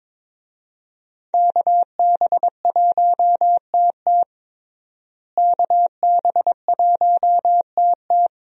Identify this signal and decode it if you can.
Morse code, heard as a keyed tone, about 22 wpm, 700 Hz. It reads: KB1TT KB1TT
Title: USA Call Signs - 5 Characters - Repeated 2x - 22wpm